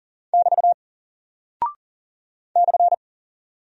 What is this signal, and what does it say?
Morse code, heard as a keyed tone, about 40 wpm, 700 Hz. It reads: <BT> /
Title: Single Character (Letter:Number:Punctuation:Prosign) Alt 40wpm